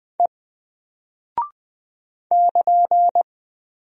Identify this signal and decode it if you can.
Morse code, heard as a keyed tone, about 20 wpm, 700 Hz. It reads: E <KN>
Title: Single Character (Letter:Number:Punctuation:Prosign) 20wpm